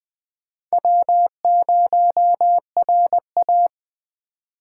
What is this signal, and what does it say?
Morse code, heard as a keyed tone, about 20 wpm, 700 Hz. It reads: W0RA
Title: Speed Racing - USA Call Signs 17wpm